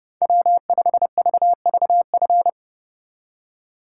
Morse code, heard as a keyed tone, about 34 wpm, 700 Hz. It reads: W5VVF
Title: Speed Racing - USA 1x3 Call Signs 30wpm